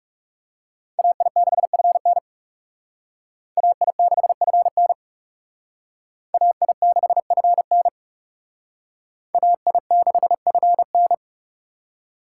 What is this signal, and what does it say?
Morse code, heard as a keyed tone, about 36 wpm, 700 Hz. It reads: AI6FN AI6FN AI6FN AI6FN
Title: Speed Racing - USA 2x2 Call Signs 30wpm